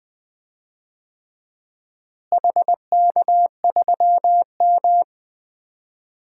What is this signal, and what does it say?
Morse code, heard as a keyed tone, about 21 wpm, 700 Hz. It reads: HK3M
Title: DXCC-100 Call Signs - 4 Characters - Repeated 3x - 20wpm